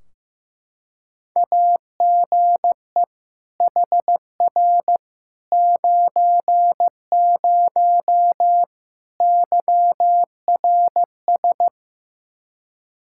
Morse code, heard as a keyed tone, about 15 wpm, 700 Hz. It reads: AGE HR 90 YRS
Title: Practice QSO - Exchange 3 - More Info - 15wpm